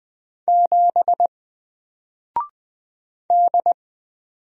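Morse code, heard as a keyed tone, about 20 wpm, 700 Hz. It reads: 7 D
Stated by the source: Single letter and number Morse drill at 20 wpm, from N4LCD.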